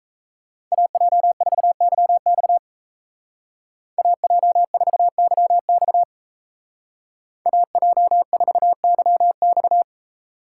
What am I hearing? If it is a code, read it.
Morse code, heard as a keyed tone, about 37 wpm, 700 Hz. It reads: AJ4YX AJ4YX AJ4YX